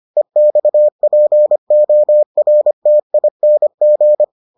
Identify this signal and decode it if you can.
Morse code, heard as a keyed tone, about 25 wpm, 600 Hz. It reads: EXPORTING